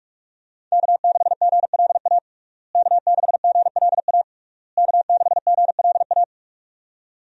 Morse code, heard as a keyed tone, about 46 wpm, 700 Hz. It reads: K6GLA K6GLA K6GLA
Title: USA Call Signs - 5 Characters - Repeated 3x - 45wpm